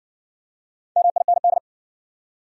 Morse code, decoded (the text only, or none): MIND